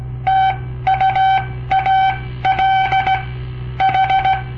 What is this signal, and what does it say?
Morse code, heard as a keyed tone, about 17 wpm, spaced as for 14 wpm, 770 Hz. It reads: TUAL H
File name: papertapes / local / Code Practice-01